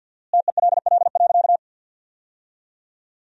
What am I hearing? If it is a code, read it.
Morse code, heard as a keyed tone, about 52 wpm, 700 Hz. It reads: TELL.
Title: 4x Word-Spacing - Sentences from Top 300 Words - 50wpm